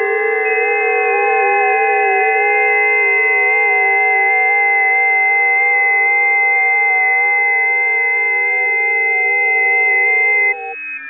FT8.
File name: tests / 20m busy / test 06